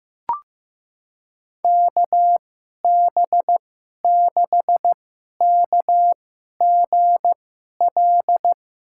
Morse code, heard as a keyed tone, about 15 wpm, spaced as for 11 wpm, 700 Hz. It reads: KB6KGL
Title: USA Call Signs - 6 Characters - 2x Character Spacing - 15wpm